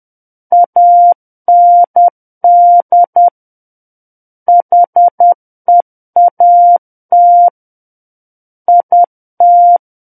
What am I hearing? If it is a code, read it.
Morse code, heard as a keyed tone, about 10 wpm, 700 Hz. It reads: AND HEAT IT
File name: Book One / 01-Chapter 1 - 10 WPM